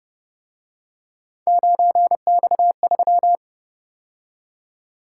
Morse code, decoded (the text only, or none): 9X3